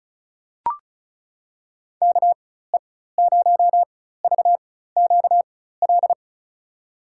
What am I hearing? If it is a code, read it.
Morse code, heard as a keyed tone, about 34 wpm, spaced as for 16 wpm, 700 Hz. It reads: KE0VQL
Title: USA Call Signs - 4x Character Spacing - 35wpm